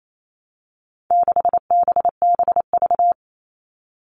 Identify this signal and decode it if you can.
Morse code, heard as a keyed tone, about 28 wpm, 700 Hz. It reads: 6BBV
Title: All Cut Numbers - 4-Digit 28wpm